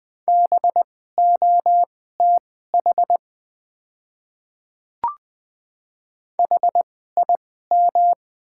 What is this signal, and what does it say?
Morse code, heard as a keyed tone, about 20 wpm, spaced as for 14 wpm, 700 Hz. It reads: BOTH HIM
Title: 2x Character Spacing - Top 300 Words - 20wpm